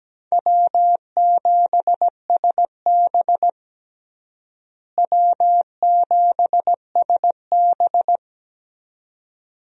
Morse code, heard as a keyed tone, about 17 wpm, 700 Hz. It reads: W7SB W7SB
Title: USA Call Signs - 4 Characters - Repeated 2x - 17wpm